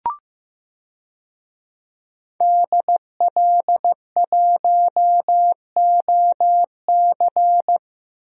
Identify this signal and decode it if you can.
Morse code, heard as a keyed tone, about 15 wpm, 700 Hz. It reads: DL1OC
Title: DXCC-100 Call Signs - 5 Characters - 15wpm